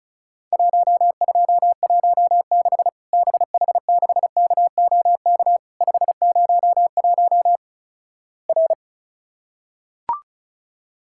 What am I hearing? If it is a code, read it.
Morse code, heard as a keyed tone, about 34 wpm, 700 Hz. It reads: 1216 BH6KOK 501 R
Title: Worked All Europe DX Contest - QTC Exchange - 35wpm